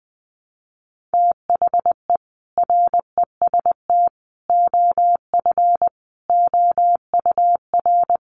Morse code, heard as a keyed tone, about 20 wpm, 700 Hz. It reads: THE REST OF OUR
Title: Sentences from Top 1400 Words 20wpm